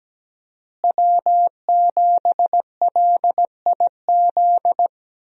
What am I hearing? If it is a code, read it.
Morse code, heard as a keyed tone, about 17 wpm, 700 Hz. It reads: W7LIZ
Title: USA Call Signs - 5 Characters - Repeated 2x - 17wpm